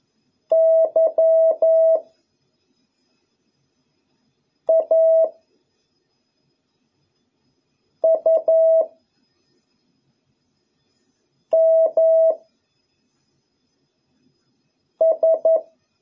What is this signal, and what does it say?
Morse code, heard as a keyed tone, about 11 wpm, 650 Hz. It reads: Y A U M S